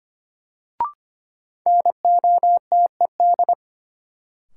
Morse code, heard as a keyed tone, about 25 wpm, 700 Hz. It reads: NOTED
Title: Sets of 5 Words - New Word Drill - 25wpm